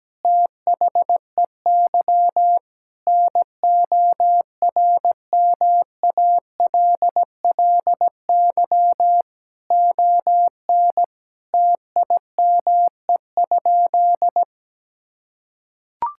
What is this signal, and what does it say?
Morse code, heard as a keyed tone, about 17 wpm, 700 Hz. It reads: THEY NORMALLY ON TIME?
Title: Sentences from Top 1000 Words - Review Part 6 - 17wpm